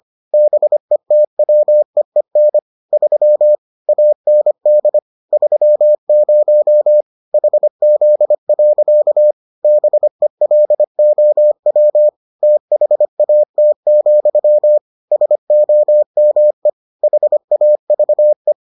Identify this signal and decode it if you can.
Morse code, heard as a keyed tone, about 25 wpm, 600 Hz. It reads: BETWEEN 3 AND 30 HZ. BELOW THAT, SOME HAVE